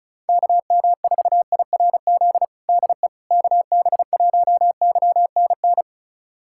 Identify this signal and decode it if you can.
Morse code, heard as a keyed tone, about 34 wpm, 700 Hz. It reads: KM4IRZ DE KB1YNN
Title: USA Callsign DE Callsign - 6 Characters - 35wpm